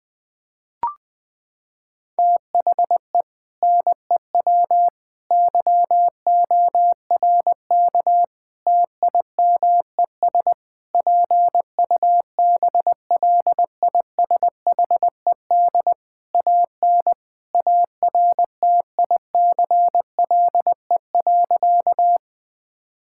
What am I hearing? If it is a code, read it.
Morse code, heard as a keyed tone, about 20 wpm, 700 Hz. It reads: THE NEW YORK TIMES PUBLISHED AN ARTICLE.
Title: Sentences with 4-Word Phrases - 20wpm